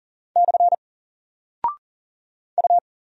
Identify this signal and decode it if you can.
Morse code, heard as a keyed tone, about 40 wpm, 700 Hz. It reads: / U
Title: Single Character (Letter:Number:Punctuation:Prosign) 40wpm